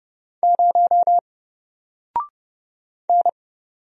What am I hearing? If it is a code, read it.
Morse code, heard as a keyed tone, about 31 wpm, 700 Hz. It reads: 0 N
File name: Single-Letter-Number-30wpm